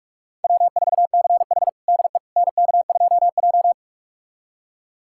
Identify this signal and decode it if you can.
Morse code, heard as a keyed tone, about 46 wpm, 700 Hz. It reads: W4CH DE NK2J
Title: USA Callsign DE Callsign - 4 Characters - 45wpm